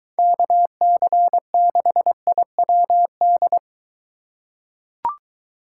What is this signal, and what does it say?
Morse code, heard as a keyed tone, about 23 wpm, 700 Hz. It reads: KC6IWD